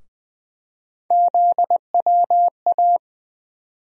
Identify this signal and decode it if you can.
Morse code, heard as a keyed tone, about 20 wpm, 700 Hz. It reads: ZWA
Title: USA Call Sign Postfixes - 3 Letters 20wpm